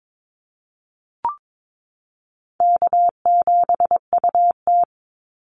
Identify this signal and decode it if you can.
Morse code, heard as a keyed tone, about 22 wpm, 700 Hz. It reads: K7UT
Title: Easy Call Signs Repeated 3x - 22wpm